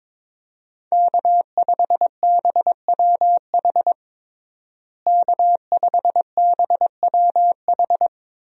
Morse code, heard as a keyed tone, about 22 wpm, 700 Hz. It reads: K5BWH K5BWH
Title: USA Call Signs - 5 Characters - Repeated 2x - 22wpm